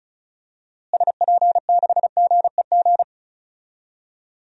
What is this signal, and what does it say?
Morse code, heard as a keyed tone, about 36 wpm, 700 Hz. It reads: SP6GEG
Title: DXCC-100 Call Signs - 6 Characters - Repeated 3x - 35wpm